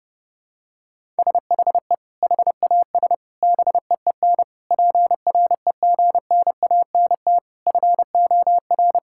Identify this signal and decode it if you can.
Morse code, heard as a keyed tone, about 30 wpm, 700 Hz. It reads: SHE HAS BEEN PREGNANT FOR
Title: Sentences from Top 1000 Words - Review Part 6 - 30wpm